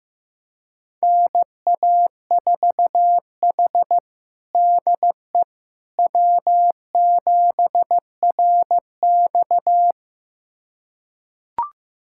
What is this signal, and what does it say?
Morse code, heard as a keyed tone, about 15 wpm, 700 Hz. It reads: NA4H DE W7RX
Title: USA Callsign DE Callsign - 4 Characters - 15wpm